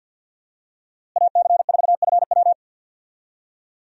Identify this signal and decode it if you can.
Morse code, heard as a keyed tone, about 52 wpm, 700 Hz. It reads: AC4FW
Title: USA Call Signs - 5 Characters - Repeated 3x - 50wpm